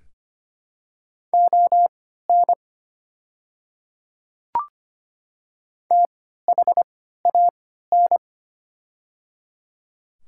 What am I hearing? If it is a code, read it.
Morse code, heard as a keyed tone, about 25 wpm, spaced as for 14 wpm, 700 Hz. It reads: ON THAN
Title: 3x Character Spacing - Top 100 Words - 25wpm